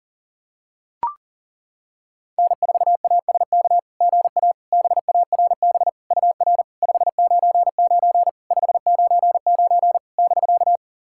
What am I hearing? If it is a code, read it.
Morse code, heard as a keyed tone, about 40 wpm, 690 Hz. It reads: N4ASK GA BARB UR 599 599 <BK>
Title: SOTA QSO Practice 40wpm